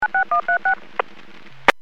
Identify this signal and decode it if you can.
DTMF, heard as telephone touch tones, tones of about 110 ms. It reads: 96136
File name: WPS / 00292 09-45 WPS 451.25 WPS